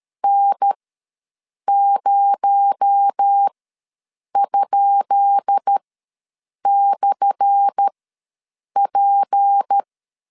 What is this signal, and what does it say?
Morse code, heard as a keyed tone, about 13 wpm, spaced as for 7 wpm, 800 Hz. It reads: N0?/P
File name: LEZIONI / 10 SIGNS / 10C DOT ws 12wpm 3548